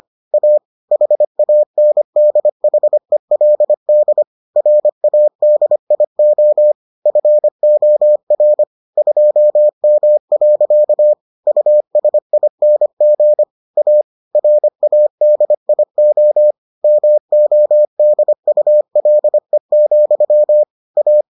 Morse code, heard as a keyed tone, about 25 wpm, 600 Hz. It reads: A HANDHELD RADIO FOR 2M. USING A RADIO MODULE, A